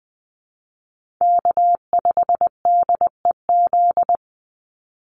Morse code, heard as a keyed tone, about 20 wpm, 690 Hz. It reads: K5DEZ
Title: NA SOTA Chaser Callsigns 20wpm